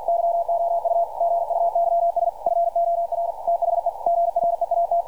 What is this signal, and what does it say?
Morse code, heard as a keyed tone, about 48 wpm, 680 Hz. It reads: YOU GUYS FOR THE GREAT
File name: archived / CRSnet